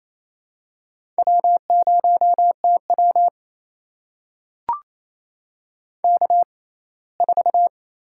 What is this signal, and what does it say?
Morse code, heard as a keyed tone, about 28 wpm, spaced as for 19 wpm, 700 Hz. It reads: W0TW K4
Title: USA Call Signs - 6x Character Spacing - 28wpm